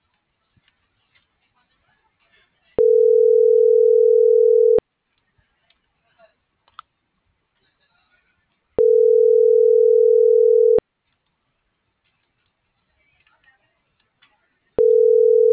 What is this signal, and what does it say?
Ringback tone (North America), 440 + 480 Hz.